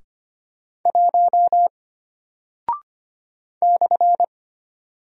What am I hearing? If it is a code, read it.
Morse code, heard as a keyed tone, about 25 wpm, 700 Hz. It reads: 1 /
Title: Single Character (Letter:Number:Punctuation:Prosign) Alt 25wpm